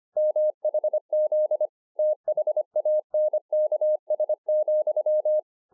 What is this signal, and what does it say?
Morse code, heard as a keyed tone, about 25 wpm, 610 Hz. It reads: MHZ THANKS,